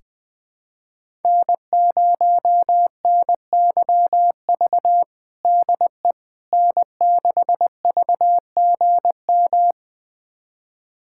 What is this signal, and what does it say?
Morse code, heard as a keyed tone, about 21 wpm, 700 Hz. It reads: N0NYV DE N6VGM
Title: USA Callsign DE Callsign - 5 Characters - 20wpm